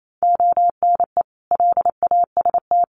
Morse code, heard as a keyed tone, about 28 wpm, 700 Hz. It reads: ONE LAST
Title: Sentences from Top 1200 Words 28wpm